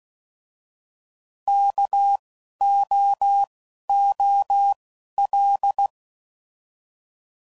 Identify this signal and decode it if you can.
Morse code, heard as a keyed tone, about 16 wpm, 800 Hz. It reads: K O O L